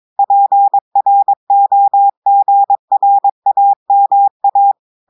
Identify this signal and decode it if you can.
Morse code, heard as a keyed tone, about 22 wpm, 800 Hz. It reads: PROGRAMA
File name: audios / portugais / portugais texte en clair 1 vitesse 22 mots minutes